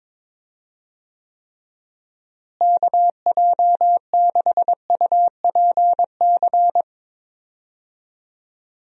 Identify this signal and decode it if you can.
Morse code, heard as a keyed tone, about 22 wpm, 700 Hz. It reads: KJ6UPC
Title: USA Call Signs - 6 Characters - Repeated 2x - 22wpm